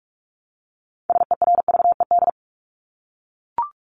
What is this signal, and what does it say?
Morse code, heard as a keyed tone, about 44 wpm, 700 Hz. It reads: SERVED
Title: ED Suffix - Top 100 Words with Base Words 45wpm